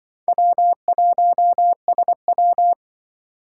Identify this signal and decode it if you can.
Morse code, heard as a keyed tone, about 24 wpm, 700 Hz. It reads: W1SW
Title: Speed Racing - USA 1x2 Call Signs 20wpm